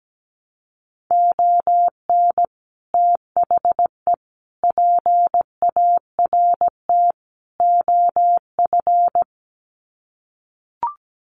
Morse code, heard as a keyed tone, about 17 wpm, 700 Hz. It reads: ON THE PART OF
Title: Sets of 4 Words 17wpm